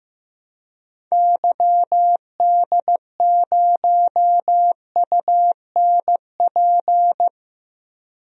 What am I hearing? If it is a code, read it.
Morse code, heard as a keyed tone, about 15 wpm, 700 Hz. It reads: YD0UNP
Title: DXCC-100 Call Signs - 6 Characters - 15wpm